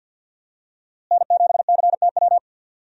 Morse code, heard as a keyed tone, about 51 wpm, 700 Hz. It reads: N7CTW
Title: USA Call Signs - 5 Characters - Repeated 3x - 50wpm